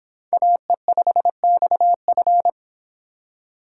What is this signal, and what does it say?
Morse code, heard as a keyed tone, about 29 wpm, 700 Hz. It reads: AE5XF